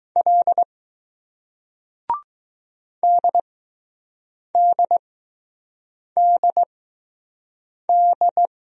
Morse code, heard as a keyed tone, about 20 wpm, spaced as for 6 wpm, 700 Hz. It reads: L DDDD